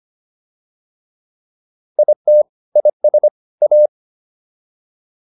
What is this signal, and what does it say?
Morse code, heard as a keyed tone, about 25 wpm, 600 Hz. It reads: IT IS A